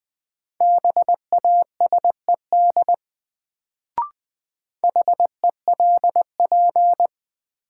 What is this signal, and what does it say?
Morse code, heard as a keyed tone, about 20 wpm, 700 Hz. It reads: BASED HELP
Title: Sets of 2 Words - New Word Drill - 20wpm